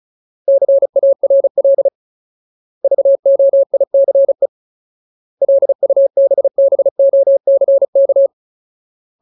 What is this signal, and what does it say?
Morse code, heard as a keyed tone, about 34 wpm, 550 Hz. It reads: CARL VOICE LUBBOCK
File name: Ultimate Word List / 35WPM Ultimate